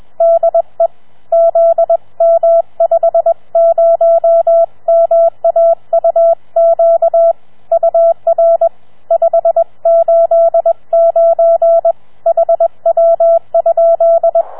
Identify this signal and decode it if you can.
Morse code, heard as a keyed tone, about 21 wpm, 670 Hz. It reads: DE ZM50MAUQ UR 589 HW?
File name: challenge-126